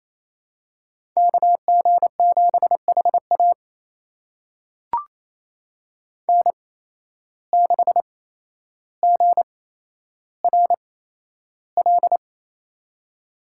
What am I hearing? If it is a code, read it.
Morse code, heard as a keyed tone, about 28 wpm, spaced as for 12 wpm, 700 Hz. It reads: KG7HA N6GRL
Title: USA Call Signs - 5 Characters - 8x Character Spacing - 28wpm